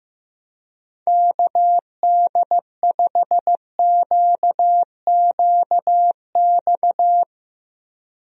Morse code, heard as a keyed tone, about 15 wpm, 700 Hz. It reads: KD5QQX
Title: USA Call Signs - 6 Characters - Repeated 2x - 15wpm